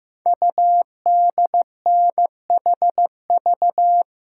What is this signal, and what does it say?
Morse code, heard as a keyed tone, about 15 wpm, 700 Hz. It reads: UDNHV
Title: Code Groups - 5 Letters - 15wpm